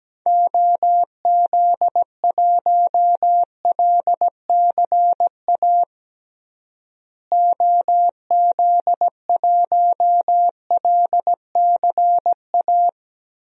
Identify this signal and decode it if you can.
Morse code, heard as a keyed tone, about 17 wpm, 700 Hz. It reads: OZ1LCA OZ1LCA
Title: DXCC-100 Call Signs - 6 Characters - Repeated 2x - 17wpm